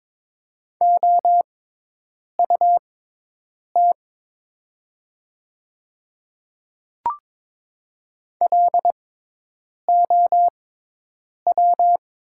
Morse code, heard as a keyed tone, about 22 wpm, spaced as for 8 wpm, 700 Hz. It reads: OUT LOW
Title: 6x Character Spacing - Top 200 Words - 22wpm